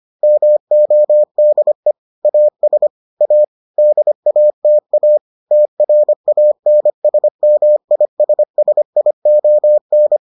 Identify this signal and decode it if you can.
Morse code, heard as a keyed tone, about 25 wpm, 600 Hz. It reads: MODE AS A DATA TRANSMISSION